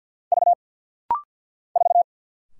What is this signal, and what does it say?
Morse code, heard as a keyed tone, about 50 wpm, 700 Hz. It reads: V 4
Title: Single Letter-Number - Rapid-Fire 50wpm